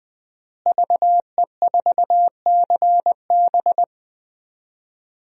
Morse code, heard as a keyed tone, about 20 wpm, 700 Hz. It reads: VE4CB
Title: DXCC-100 Call Signs - 5 Characters - Repeated 3x - 20wpm